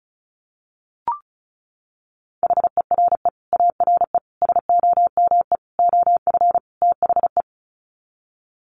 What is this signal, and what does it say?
Morse code, heard as a keyed tone, about 34 wpm, 700 Hz. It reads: HERE ARE SOME OF THE
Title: Sets of 5 Words Encore 35wpm